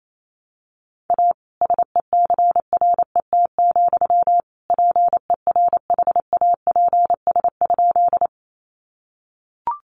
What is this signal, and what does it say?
Morse code, heard as a keyed tone, about 28 wpm, 700 Hz. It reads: A SECRET, PERHAPS?
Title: Sentences from Top 1200 Words 28wpm